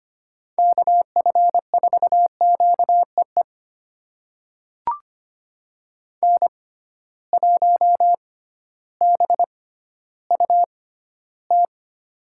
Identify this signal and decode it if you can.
Morse code, heard as a keyed tone, about 25 wpm, spaced as for 14 wpm, 700 Hz. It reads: KF4QEE N1BUT
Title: USA Call Signs - 6x Character Spacing - 25wpm